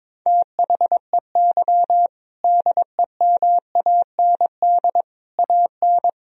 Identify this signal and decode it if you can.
Morse code, heard as a keyed tone, about 21 wpm, 700 Hz. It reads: THEY DEMAND AN
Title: Sentences from Top 1000 Words - Review Part 4 - 22wpm